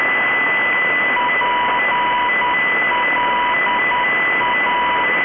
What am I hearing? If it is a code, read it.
Morse code, heard as a keyed tone, about 10 wpm, 980 Hz. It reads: PLA